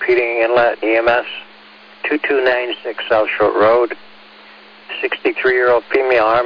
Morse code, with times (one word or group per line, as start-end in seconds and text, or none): none